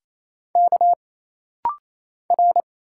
Morse code, heard as a keyed tone, about 27 wpm, 700 Hz. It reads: K R
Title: Single Letter-Number - Rapid-Fire 28wpm